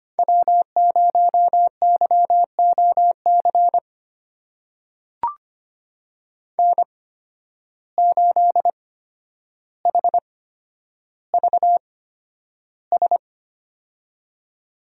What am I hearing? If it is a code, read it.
Morse code, heard as a keyed tone, about 25 wpm, spaced as for 11 wpm, 700 Hz. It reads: W0YOC N8HVS
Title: USA Call Signs - 5 Characters - 8x Character Spacing - 25wpm